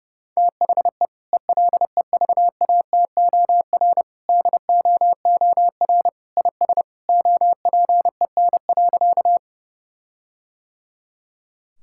Morse code, heard as a keyed tone, about 30 wpm, 700 Hz. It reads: THE ELEVATOR DOOR IS OPEN.
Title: Sentences from Top 1600 Words 30wpm